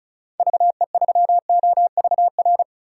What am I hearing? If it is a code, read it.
Morse code, heard as a keyed tone, about 36 wpm, 700 Hz. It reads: VE3OVR
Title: DXCC-100 Call Signs - 6 Characters - 35wpm